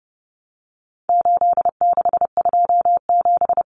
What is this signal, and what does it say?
Morse code, heard as a keyed tone, about 30 wpm, 700 Hz. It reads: 8627
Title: Numbers - 4-Digit 30wpm